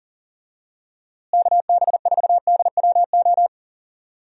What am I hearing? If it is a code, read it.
Morse code, heard as a keyed tone, about 40 wpm, 700 Hz. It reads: KB4DWO
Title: USA Call Signs - 6 Characters - Repeated 2x - 40wpm